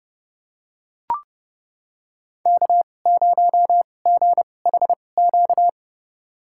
Morse code, heard as a keyed tone, about 30 wpm, spaced as for 22 wpm, 700 Hz. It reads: K0GHQ
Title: USA Call Signs - 5 Characters - 2x Character Spacing - 30wpm